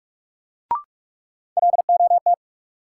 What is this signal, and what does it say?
Morse code, heard as a keyed tone, about 41 wpm, 700 Hz. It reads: LOT
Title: Sentences from Top 500 Words - New Word Drill - 45wpm